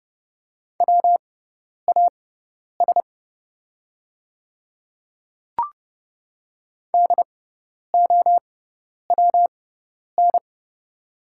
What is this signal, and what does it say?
Morse code, heard as a keyed tone, about 30 wpm, spaced as for 10 wpm, 700 Hz. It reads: WAS DOWN
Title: 6x Character Spacing - Top 100 Words - 30wpm